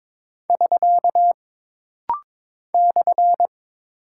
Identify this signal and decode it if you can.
Morse code, heard as a keyed tone, about 22 wpm, 700 Hz. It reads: <SK> /
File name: practice / 22wpm / Single Punctuation-Prosign - Rapid-Fire - 22wpm